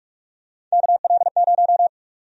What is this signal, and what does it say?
Morse code, heard as a keyed tone, about 43 wpm, 700 Hz. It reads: KL0
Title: USA Call Sign Prefixes - 45wpm